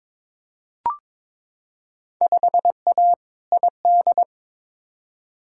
Morse code, heard as a keyed tone, about 22 wpm, 700 Hz. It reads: 5A ID
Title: Field Day Exchanges Encore - 22wpm